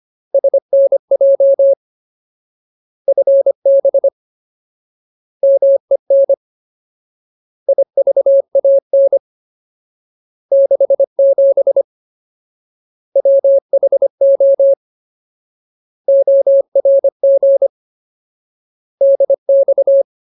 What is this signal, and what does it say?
Morse code, heard as a keyed tone, about 25 wpm, 550 Hz. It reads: SNJ FB MEN IVAN 67 WHO ORG DX
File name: Ultimate Word List / 25WPM Ultimate